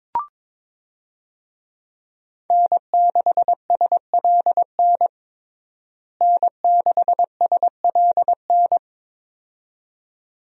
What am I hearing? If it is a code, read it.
Morse code, heard as a keyed tone, about 21 wpm, 700 Hz. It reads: N6SLN N6SLN
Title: USA Call Signs - 5 Characters - Repeated 2x - 22wpm